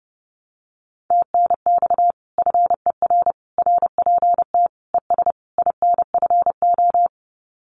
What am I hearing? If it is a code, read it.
Morse code, heard as a keyed tone, about 30 wpm, 700 Hz. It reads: TNX FER RPT ES INFO
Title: Common QSO Phrases Encore 30wpm